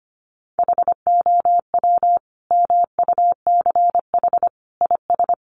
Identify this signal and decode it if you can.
Morse code, heard as a keyed tone, about 25 wpm, 700 Hz. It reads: HOW MUCH IS
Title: Sentences from Top 1100 Words 25wpm